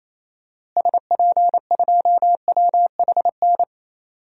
Morse code, heard as a keyed tone, about 28 wpm, 700 Hz. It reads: SP2WHN
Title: DXCC-100 Call Signs - 6 Characters - Repeated 2x - 28wpm